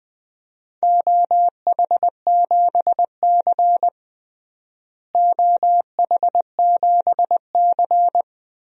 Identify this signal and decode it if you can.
Morse code, heard as a keyed tone, about 21 wpm, 700 Hz. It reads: OH7C OH7C
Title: DXCC-100 Call Signs - 4 Characters - Repeated 3x - 20wpm